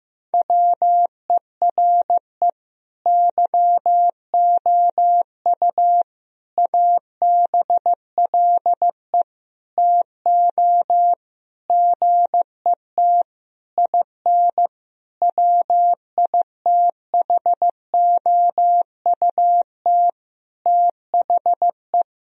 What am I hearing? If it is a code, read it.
Morse code, heard as a keyed tone, about 16 wpm, 700 Hz. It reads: WERE YOU ABLE TO GET IN WITHOUT THE
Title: Sentences from Top 1000 Words - Review Part 4 - 15wpm